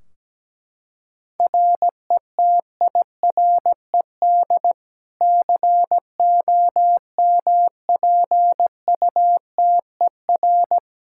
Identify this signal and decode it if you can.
Morse code, heard as a keyed tone, about 17 wpm, 700 Hz. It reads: RETIRED COMPUTER
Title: Practice QSO - Exchange 3 - More Info - 17wpm